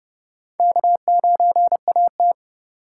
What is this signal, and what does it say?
Morse code, heard as a keyed tone, about 29 wpm, 700 Hz. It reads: K9AT